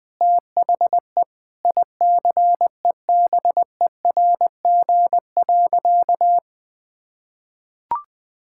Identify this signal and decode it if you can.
Morse code, heard as a keyed tone, about 20 wpm, 700 Hz. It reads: THE ICEBERG.
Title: Sentences with 7-Word Phrases - 20wpm